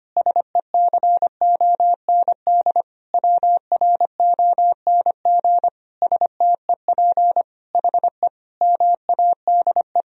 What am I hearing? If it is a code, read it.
Morse code, heard as a keyed tone, about 25 wpm, 700 Hz. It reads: SECOND WRONG STEP HE MADE
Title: Sentences from Top 1000 Words - Review Part 2 - 25wpm